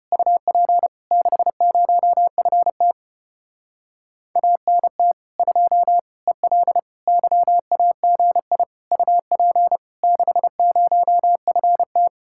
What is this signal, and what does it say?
Morse code, heard as a keyed tone, about 33 wpm, 700 Hz. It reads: UP 60FT ANT 2 EL YAGI UP 60FT